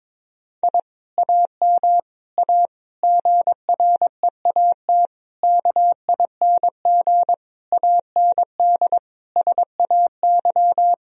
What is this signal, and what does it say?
Morse code, heard as a keyed tone, about 22 wpm, 700 Hz. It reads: I AM A GREAT KING AND SAY
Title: Sentences from Top 1000 Words - Review Part 4 - 22wpm